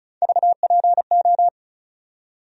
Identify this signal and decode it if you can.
Morse code, heard as a keyed tone, about 34 wpm, 700 Hz. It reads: VPO
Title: USA Call Sign Postfixes - 3 Letters 35wpm